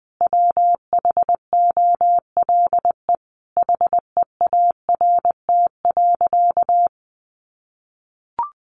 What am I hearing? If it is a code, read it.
Morse code, heard as a keyed tone, about 20 wpm, 700 Hz. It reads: WHOLE HEART.
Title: Sentences from Top 1200 Words 20wpm